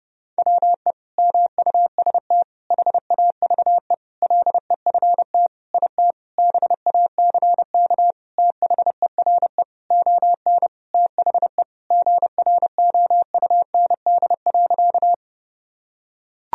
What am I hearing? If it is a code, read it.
Morse code, heard as a keyed tone, about 32 wpm, 700 Hz. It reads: WE MUST HAVE LEFT IT BACK THERE ON THE GROUND.
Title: Sentences from Top 1000 Words - Review Part 2 - 30wpm